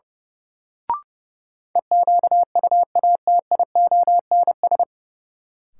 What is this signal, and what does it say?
Morse code, heard as a keyed tone, about 31 wpm, 700 Hz. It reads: EQUATIONS
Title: Sets of 5 Words - New Word Drill - 30wpm